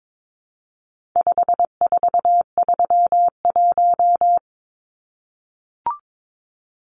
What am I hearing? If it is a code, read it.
Morse code, heard as a keyed tone, about 22 wpm, 700 Hz. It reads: 5431
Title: Numbers - 4-Digit 22wpm